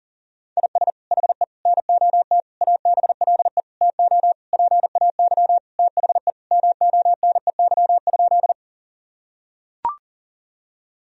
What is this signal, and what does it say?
Morse code, heard as a keyed tone, about 40 wpm, 700 Hz. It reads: IS HE NOT ABLE TO PAY THE MONEY?
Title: Sentences from Top 1000 Words - Review Part 2 - 40wpm